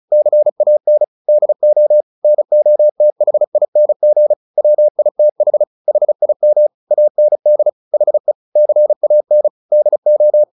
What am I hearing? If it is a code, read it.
Morse code, heard as a keyed tone, about 34 wpm, 600 Hz. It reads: CAN DO NOTHING WITH HIM AND HE CAN DO
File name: the old man and the sea / the old man and the sea 1 35wpm